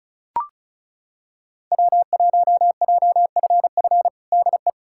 Morse code, heard as a keyed tone, about 36 wpm, 700 Hz. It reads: W1JFF DE
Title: USA Callsign DE Callsign 35wpm